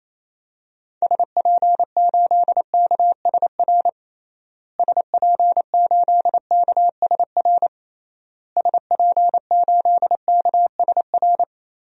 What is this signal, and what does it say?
Morse code, heard as a keyed tone, about 28 wpm, 700 Hz. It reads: SP8KSR SP8KSR SP8KSR
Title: DXCC-100 Call Signs - 6 Characters - Repeated 3x - 28wpm